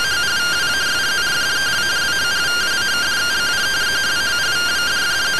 Two-tone teleprinter signal, about 170 Hz shift.
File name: findings / CT AXIAL -142.gen-15 rtty rx